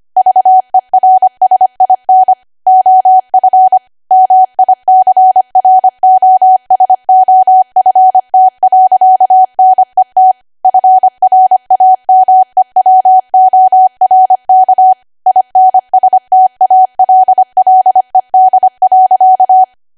Morse code, heard as a keyed tone, about 25 wpm, 750 Hz. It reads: VERSIN OF MICROSOFT.NET FRAMEWORK INSTALLED.
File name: Archive / 25 WPM / 25WPM